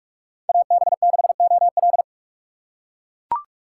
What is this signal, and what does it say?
Morse code, heard as a keyed tone, about 39 wpm, 700 Hz. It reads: AB6OL